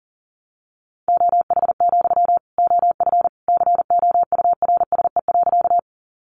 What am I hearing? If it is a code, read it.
Morse code, heard as a keyed tone, about 40 wpm, 700 Hz. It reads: OH, OF COURSE.
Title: Sentences from Top 500 Words 40wpm